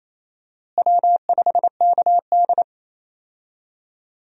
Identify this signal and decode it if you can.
Morse code, heard as a keyed tone, about 28 wpm, 700 Hz. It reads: W5KD
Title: USA Call Signs - 4 Characters - Repeated 3x - 28wpm